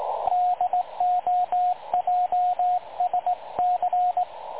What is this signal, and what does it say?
Morse code, heard as a keyed tone, about 20 wpm, 720 Hz, with radio noise. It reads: DOJSC
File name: dlazaawansowanych / nagraniazpasma / websdr recording 3565.0kHz